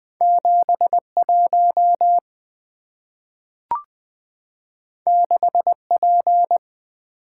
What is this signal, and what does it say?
Morse code, heard as a keyed tone, about 20 wpm, 690 Hz. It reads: 71 6P
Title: Two Letter-Number Combinations - 20wpm